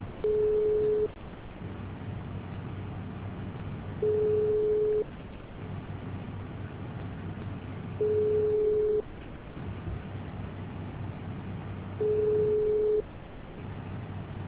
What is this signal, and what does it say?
Ringback tone (425 Hz, 1 s on, 3 s off).